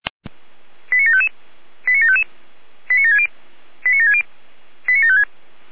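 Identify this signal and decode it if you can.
Five-tone selective call (ZVEI), heard as quick runs of five tones, tones of about 80 ms. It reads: 89755 89755 89760 89760 89756
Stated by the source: ZVEI five-tone selective call received on an SDR.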